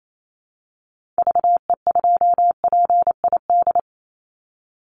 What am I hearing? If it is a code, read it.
Morse code, heard as a keyed tone, about 28 wpm, 700 Hz. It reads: VE2PID
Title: Call Signs Repeated 3x - 28wpm